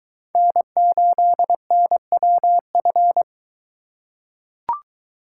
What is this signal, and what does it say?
Morse code, heard as a keyed tone, about 22 wpm, 700 Hz. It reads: N8NWF